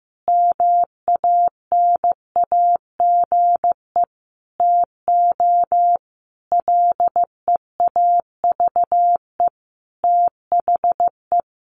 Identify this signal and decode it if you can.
Morse code, heard as a keyed tone, about 15 wpm, 700 Hz. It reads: MANAGE TO LEAVE THE
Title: Sentences from Top 1400 Words 15wpm